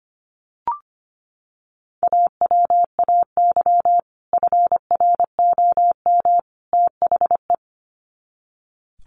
Morse code, heard as a keyed tone, about 25 wpm, 700 Hz. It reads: AWAY FROM THE
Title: Sets of 3 Words 25wpm